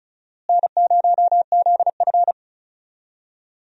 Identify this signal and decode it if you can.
Morse code, heard as a keyed tone, about 34 wpm, 700 Hz. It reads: N0ZF
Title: USA Call Signs - 4 Characters - Repeated 3x - 35wpm